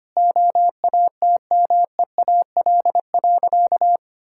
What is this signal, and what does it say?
Morse code, heard as a keyed tone, about 25 wpm, 690 Hz. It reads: OATMEAL.
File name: Easy Sentences with Hints - 25wpm